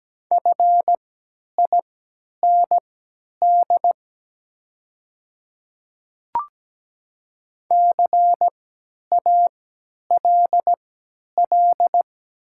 Morse code, heard as a keyed tone, about 17 wpm, spaced as for 10 wpm, 700 Hz. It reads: FIND CALL
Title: 3x Character Spacing - Top 200 Words - 17wpm